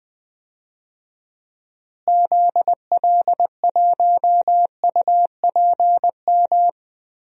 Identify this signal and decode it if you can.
Morse code, heard as a keyed tone, about 20 wpm, 700 Hz. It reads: ZL1UPM
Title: DXCC-100 Call Signs - 6 Characters - Repeated 3x - 20wpm